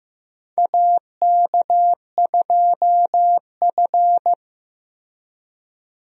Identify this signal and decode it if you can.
Morse code, heard as a keyed tone, about 15 wpm, 700 Hz. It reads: AK2F
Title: USA Call Signs - 4 Characters - 15wpm